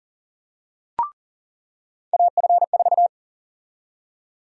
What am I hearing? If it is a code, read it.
Morse code, heard as a keyed tone, about 40 wpm, 700 Hz. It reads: AF4
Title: USA Call Sign Prefixes - 40wpm